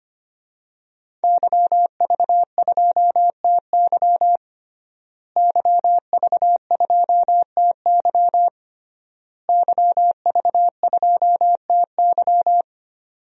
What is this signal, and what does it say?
Morse code, heard as a keyed tone, about 25 wpm, 700 Hz. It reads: YV2TY YV2TY YV2TY
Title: DXCC-100 Call Signs - 5 Characters - Repeated 3x - 25wpm